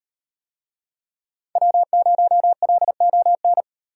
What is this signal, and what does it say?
Morse code, heard as a keyed tone, about 38 wpm, 700 Hz. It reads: W0LON